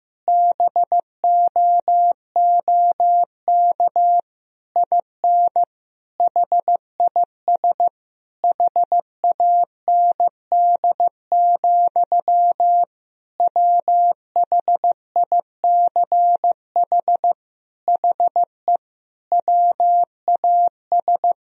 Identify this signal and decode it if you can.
Morse code, heard as a keyed tone, about 15 wpm, 700 Hz. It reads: BOOK IN HIS HAND, WHICH HE WAS
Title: Sentences from Top 1000 Words - Review Part 3 - 15wpm